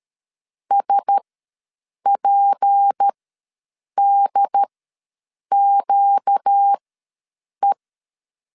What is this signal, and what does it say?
Morse code, heard as a keyed tone, about 13 wpm, spaced as for 7 wpm, 800 Hz. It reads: SPDQE
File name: LEZIONI / 6 PKX0 / 6 P ws 12wpm 3117